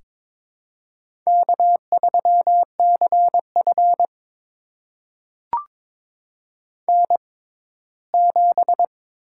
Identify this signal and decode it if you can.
Morse code, heard as a keyed tone, about 22 wpm, spaced as for 15 wpm, 700 Hz. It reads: K3CF N7
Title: USA Call Signs - 6x Character Spacing - 22wpm